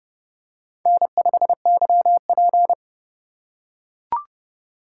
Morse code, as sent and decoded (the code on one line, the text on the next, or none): -. ..... -.-- .--.
N5YP